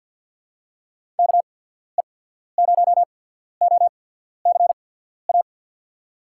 Morse code, heard as a keyed tone, about 52 wpm, spaced as for 14 wpm, 700 Hz. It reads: KE0OCA
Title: USA Call Signs - 6 Characters - 8x Character Spacing - 50wpm